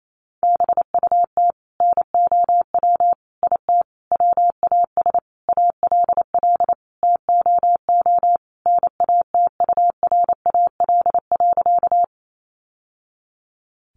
Morse code, heard as a keyed tone, about 28 wpm, 700 Hz. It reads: BUT NOW IT WAS ALL TOO NATURAL.
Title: Sentences from Top 700 Words 28wpm